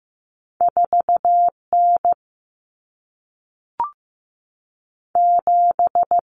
Morse code, heard as a keyed tone, about 15 wpm, 700 Hz. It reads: 4N 7
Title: Common Cut Numbers - 2-Digit 15wpm